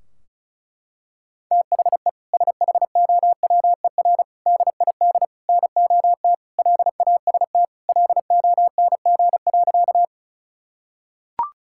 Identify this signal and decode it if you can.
Morse code, heard as a keyed tone, about 36 wpm, 700 Hz. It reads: THE SHOWER DID NOT LAST LONG.
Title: Sentences from Top 1600 Words 35wpm